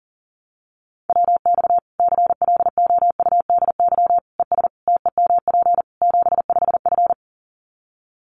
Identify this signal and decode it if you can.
Morse code, heard as a keyed tone, about 40 wpm, 700 Hz. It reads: WX CLOUDY ES TEMP 75F
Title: Common QSO Phrases 40wpm